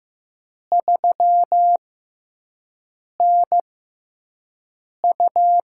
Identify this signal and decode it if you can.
Morse code, heard as a keyed tone, about 15 wpm, spaced as for 5 wpm, 700 Hz. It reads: 3NU
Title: USA Call Signs - 4 Characters - 6x Character Spacing - 15wpm